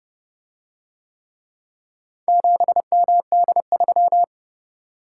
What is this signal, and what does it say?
Morse code, heard as a keyed tone, about 30 wpm, 700 Hz. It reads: 7MD3